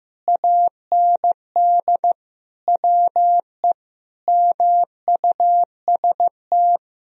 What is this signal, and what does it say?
Morse code, heard as a keyed tone, about 15 wpm, 700 Hz. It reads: AND WE MUST
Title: Sentences from Top 1000 Words - Review Part 1 - 15wpm